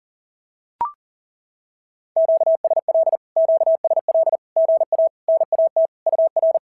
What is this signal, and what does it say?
Morse code, heard as a keyed tone, about 43 wpm, 650 Hz. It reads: QSL QSL GA NAT UR
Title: SOTA QSO Practice 40wpm